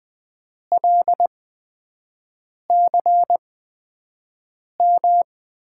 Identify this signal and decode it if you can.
Morse code, heard as a keyed tone, about 20 wpm, spaced as for 5 wpm, 700 Hz. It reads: LCM
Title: USA Call Signs - 6 Characters - 8x Character Spacing - 20wpm